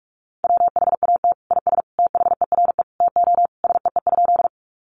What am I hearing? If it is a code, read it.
Morse code, heard as a keyed tone, about 44 wpm, 700 Hz. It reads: WHAT IS THERE TO SEE?
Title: Sentences from Top 100 Words 45wpm